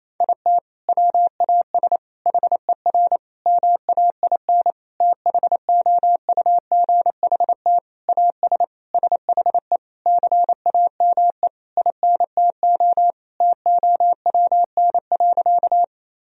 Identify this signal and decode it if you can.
Morse code, heard as a keyed tone, about 28 wpm, 700 Hz. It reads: IT WAS HER MAIN THOUGHT AS SHE CAME INTO TOWN.
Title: Sentences from Top 1000 Words - Review Part 2 - 28wpm